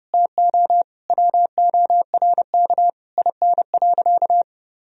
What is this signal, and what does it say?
Morse code, heard as a keyed tone, about 30 wpm, 700 Hz. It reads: TO WORK IN.
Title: Sentences from Top 1000 Words - Review Part 4 - 30wpm